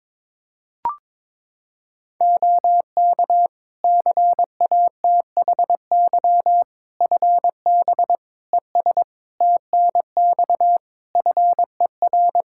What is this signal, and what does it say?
Morse code, heard as a keyed tone, about 22 wpm, 700 Hz. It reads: OK CATHY FB ES TNX FER
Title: K1USN SST QSO Practice 22wpm